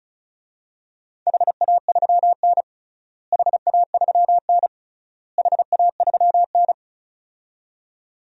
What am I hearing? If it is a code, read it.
Morse code, heard as a keyed tone, about 36 wpm, 700 Hz. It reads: HA3N HA3N HA3N
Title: DXCC-100 Call Signs - 4 Characters - Repeated 3x - 35wpm